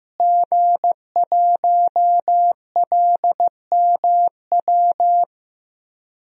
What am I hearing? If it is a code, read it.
Morse code, heard as a keyed tone, about 15 wpm, 700 Hz. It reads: G1LMW
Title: DXCC-100 Call Signs - 5 Characters - Repeated 2x - 15wpm